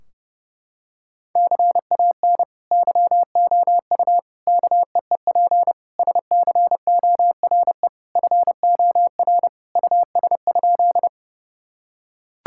Morse code, heard as a keyed tone, about 30 wpm, 700 Hz. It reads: CAN YOU KEEP SCORE FOR US?
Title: Sentences from Top 1600 Words 30wpm